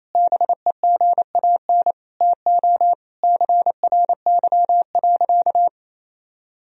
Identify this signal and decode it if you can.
Morse code, heard as a keyed tone, about 27 wpm, 700 Hz. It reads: BEGAN TO CRY.
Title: Sentences from Top 1000 Words - Review Part 2 - 28wpm